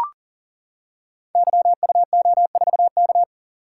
Morse code, heard as a keyed tone, about 38 wpm, 700 Hz. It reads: YUO4K
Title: Code Groups - 5 Letter:Numbers - 40wpm